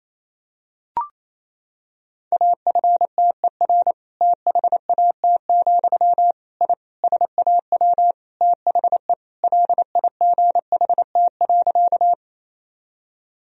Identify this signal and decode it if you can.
Morse code, heard as a keyed tone, about 28 wpm, 700 Hz. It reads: AFTER THAT, I SAW THE LIGHT.
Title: Sentences from Top 1000 Words - Review Part 1 - 28wpm